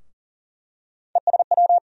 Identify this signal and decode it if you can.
Morse code, heard as a keyed tone, about 40 wpm, 700 Hz. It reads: ESW
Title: USA Call Sign Postfixes - 3 Letters 40wpm